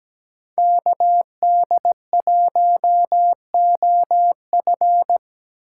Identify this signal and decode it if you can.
Morse code, heard as a keyed tone, about 19 wpm, 700 Hz. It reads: KD1OF
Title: Speed Racing - USA 2x2 Call Signs 17wpm